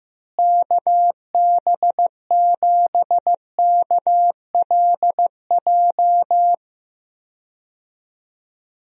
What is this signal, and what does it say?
Morse code, heard as a keyed tone, about 15 wpm, 700 Hz. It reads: KB7KLJ
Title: USA Call Signs - 6 Characters - Repeated 2x - 15wpm